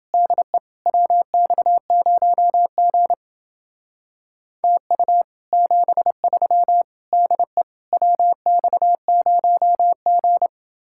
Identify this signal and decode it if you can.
Morse code, heard as a keyed tone, about 28 wpm, 700 Hz. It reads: DE WX0G TU 73 DE WX0G